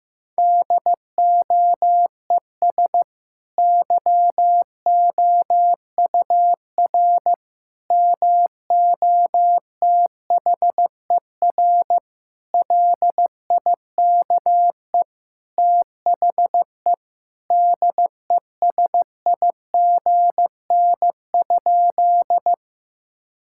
Morse code, heard as a keyed tone, about 15 wpm, 700 Hz. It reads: DOES YOUR MOTHER LIKE THE DESIGN?
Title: Sentences from Top 1000 Words - Review Part 5 - 15wpm